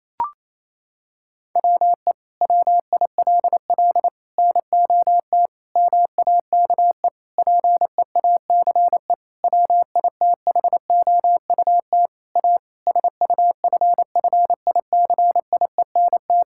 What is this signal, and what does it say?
Morse code, heard as a keyed tone, about 28 wpm, 700 Hz. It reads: WE WILL NOT MAKE PEACE WITHOUT A SUFFICIENT
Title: Sentences from Top 1600 Words 28wpm